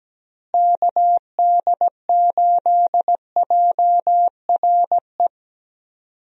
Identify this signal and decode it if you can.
Morse code, heard as a keyed tone, about 17 wpm, 700 Hz. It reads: KD8JRE
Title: USA Call Signs - 6 Characters - Repeated 3x - 17wpm